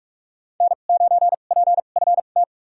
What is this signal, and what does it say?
Morse code, heard as a keyed tone, about 46 wpm, spaced as for 32 wpm, 700 Hz. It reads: N9PFT
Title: USA Call Signs - 5 Characters - 2x Character Spacing - 45wpm